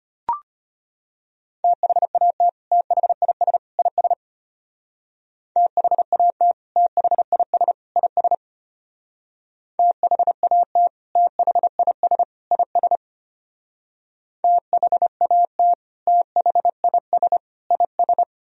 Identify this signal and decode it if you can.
Morse code, heard as a keyed tone, about 31 wpm, 700 Hz. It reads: THAT THIS IS THAT THIS IS THAT THIS IS THAT THIS IS
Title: Speed-Racing - Sets of 3 Words 25wpm